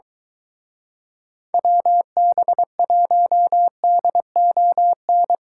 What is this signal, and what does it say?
Morse code, heard as a keyed tone, about 23 wpm, 700 Hz. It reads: WB1DON